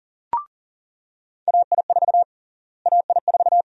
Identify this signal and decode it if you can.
Morse code, heard as a keyed tone, about 40 wpm, 700 Hz. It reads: AI4 AI4
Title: USA Call Sign Prefixes - Repeated 2x - 40wpm